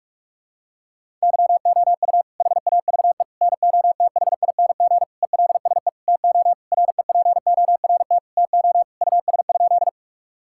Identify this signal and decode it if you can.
Morse code, heard as a keyed tone, about 41 wpm, 690 Hz. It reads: YOU HAVE NOTHING ELSE TO REPORT TO US?
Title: Sentences from Top 1000 Words - Review Part 6 - 45wpm